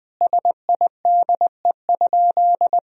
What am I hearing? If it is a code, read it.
Morse code, heard as a keyed tone, about 20 wpm, 700 Hz. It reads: SIDE?
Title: Sentences from Top 1000 Words - Review Part 3 - 20wpm